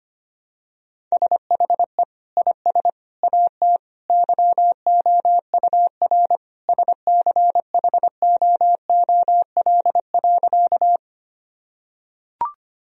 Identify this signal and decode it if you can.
Morse code, heard as a keyed tone, about 25 wpm, 700 Hz. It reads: SHE IS AT YOUR SCHOOL.
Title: Sentences from Top 1000 Words - Review Part 1 - 25wpm